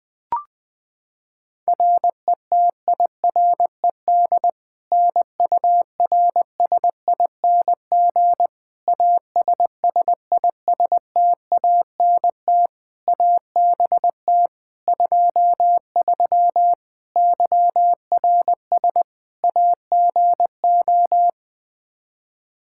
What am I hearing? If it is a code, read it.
Morse code, heard as a keyed tone, about 21 wpm, 700 Hz. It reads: RETIRED NURSING ASSISTANT ABT 23 YRS AGO
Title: Practice QSO - Exchange 3 - More Info - 20wpm